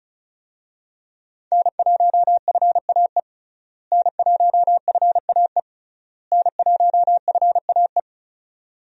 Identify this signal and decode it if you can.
Morse code, heard as a keyed tone, about 34 wpm, 700 Hz. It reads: N1FAE N1FAE N1FAE
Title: USA Call Signs - 5 Characters - Repeated 3x - 35wpm